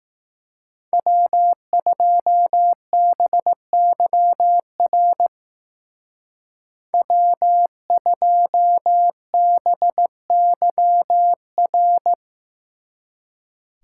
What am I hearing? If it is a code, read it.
Morse code, heard as a keyed tone, about 18 wpm, 700 Hz. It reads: W2BYR W2BYR
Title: Speed Racing - USA 1x3 Call Signs 15wpm